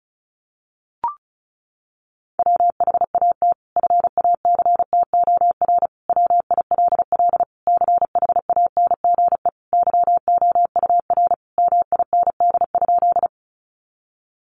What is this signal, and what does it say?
Morse code, heard as a keyed tone, about 34 wpm, 700 Hz. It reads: WHAT FACTOR WILL CHANGE YOUR MIND?
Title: Sentences from Top 1300 Words 35wpm